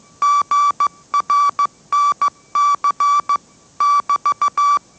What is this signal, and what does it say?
Morse code, heard as a keyed tone, about 15 wpm, 1160 Hz. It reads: GRNC <BT>
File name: mrx / mrx065 16wpm